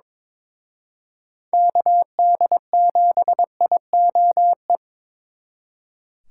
Morse code, heard as a keyed tone, about 25 wpm, 700 Hz. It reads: KD7IOE
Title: Speed Racing - USA 2x3 Call Signs 22wpm